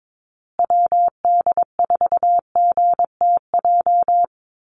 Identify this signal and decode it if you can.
Morse code, heard as a keyed tone, about 22 wpm, 700 Hz. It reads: WD4GTJ
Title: USA Call Signs Repeated 2x 22wpm